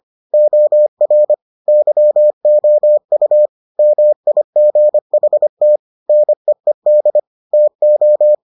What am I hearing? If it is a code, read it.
Morse code, heard as a keyed tone, about 25 wpm, 600 Hz. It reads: OR YOU MIGHT NEED TO